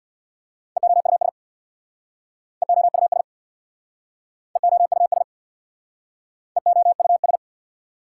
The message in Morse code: EOUS EOUS EOUS EOUS

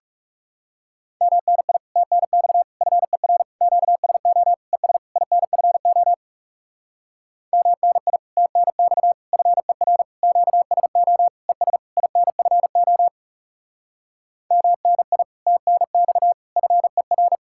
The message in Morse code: MNI TNX FER QSO ES INFO MNI TNX FER QSO ES INFO MNI TNX FER

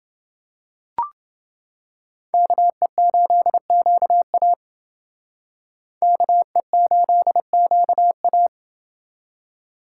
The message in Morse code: KE8QA KE8QA

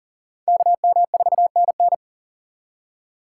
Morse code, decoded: KM4NN

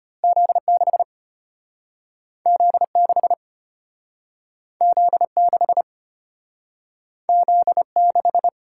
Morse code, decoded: Z6 Z6 Z6 Z6

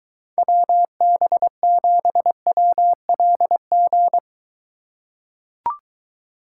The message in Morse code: WB7WLG

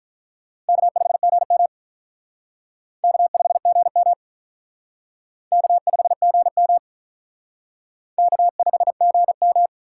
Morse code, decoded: K5GM K5GM K5GM K5GM

